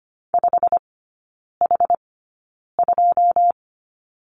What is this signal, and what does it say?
Morse code, heard as a keyed tone, about 25 wpm, spaced as for 9 wpm, 690 Hz. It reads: 5H2
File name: Lesson 15e - Number 5 - Three Characters 25wpm